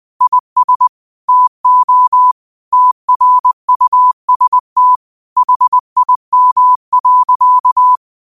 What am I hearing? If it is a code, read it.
Morse code, heard as a keyed tone, about 20 wpm, 1000 Hz. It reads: IS TO TRUST HIM.
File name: qotdfiles / 20wpm